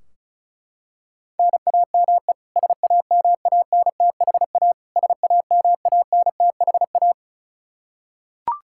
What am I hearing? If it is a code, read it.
Morse code, heard as a keyed tone, about 36 wpm, 700 Hz. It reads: NAME SAMANTHA SAMANTHA
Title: Practice QSO - Exchange 1 - Beginning - 35wpm